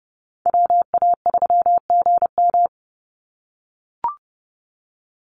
Morse code, heard as a keyed tone, about 30 wpm, 700 Hz. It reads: WA3GM
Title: Call Signs - 30wpm